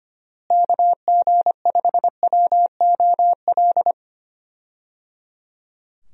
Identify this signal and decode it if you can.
Morse code, heard as a keyed tone, about 25 wpm, 700 Hz. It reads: KG5WOL
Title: USA Call Signs - 6 Characters - Repeated 3x - 25wpm